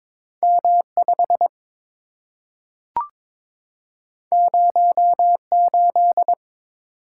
Morse code, heard as a keyed tone, about 22 wpm, 690 Hz. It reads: M5 08
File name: practice / 22wpm / Two Letter-Number Combinations - 22wpm